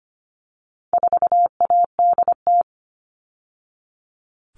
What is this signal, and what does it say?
Morse code, heard as a keyed tone, about 25 wpm, 700 Hz. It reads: 4ADT
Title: All Cut Numbers - 4-Digit 25wpm